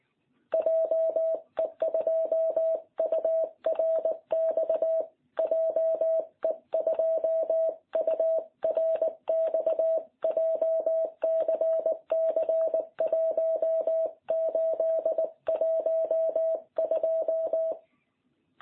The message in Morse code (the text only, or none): JE2URX JE2URX JCC 1812